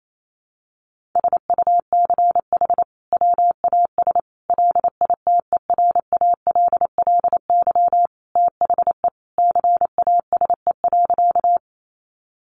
Morse code, heard as a keyed tone, about 28 wpm, 700 Hz. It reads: SUCH WAS LITERALLY THE CASE.
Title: Sentences from Top 1400 Words 28wpm